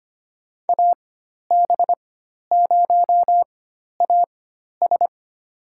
Morse code, heard as a keyed tone, about 25 wpm, spaced as for 12 wpm, 700 Hz. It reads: AB0AS
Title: USA Call Signs - 5 Characters - 4x Character Spacing - 25wpm